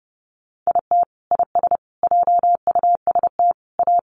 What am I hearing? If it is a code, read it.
Morse code, heard as a keyed tone, about 30 wpm, 700 Hz. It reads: IT IS JUST A
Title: Sentences from Top 1200 Words 30wpm